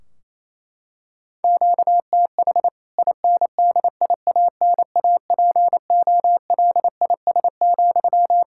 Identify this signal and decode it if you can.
Morse code, heard as a keyed tone, about 28 wpm, 700 Hz. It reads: QTH INDIANAPOLIS,
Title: K1USN SST QSO Practice 28wpm